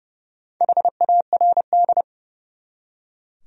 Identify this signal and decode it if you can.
Morse code, heard as a keyed tone, about 36 wpm, 700 Hz. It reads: HARD
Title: Speed-Racing Top 300 Words 30wpm